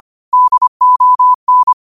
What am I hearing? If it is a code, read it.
Morse code, heard as a keyed tone, about 25 wpm, 1000 Hz. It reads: DON
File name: qotdfiles / 25wpm